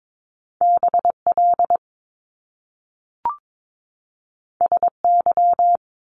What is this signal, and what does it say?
Morse code, heard as a keyed tone, about 22 wpm, 700 Hz. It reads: BL SY
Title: Top Two Letter Combinations in Words 22wpm